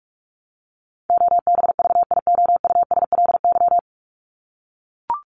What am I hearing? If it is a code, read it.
Morse code, heard as a keyed tone, about 44 wpm, 700 Hz. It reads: OBVIOUSLY
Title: Top 1501-1750 Words 45wpm